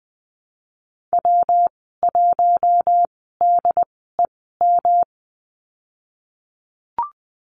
Morse code, heard as a keyed tone, about 20 wpm, spaced as for 15 wpm, 700 Hz. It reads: W1DEM
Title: 219 Lesson 35g - Number 0 - Call Signs 20wpm